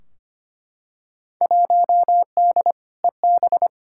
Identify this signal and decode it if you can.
Morse code, heard as a keyed tone, about 25 wpm, 700 Hz. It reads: 1D EB
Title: Field Day Exchanges - 25wpm